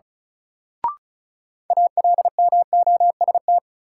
Morse code, heard as a keyed tone, about 36 wpm, 700 Hz. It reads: ALMOST
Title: Sentences from Top 700 Words - New Word Drill - 35wpm